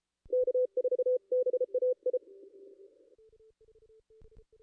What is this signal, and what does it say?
Morse code, heard as a keyed tone, about 33 wpm, 500 Hz. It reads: K4BAI
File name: contests / ss-cw-k5tr-op-n5rz -2200